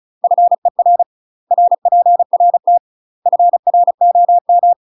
Morse code, heard as a keyed tone, about 35 wpm, 700 Hz. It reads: FER RPRT FROM
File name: UK MASTER QSOs TRAINING / 35WPM / UK MASTER QSOs 0042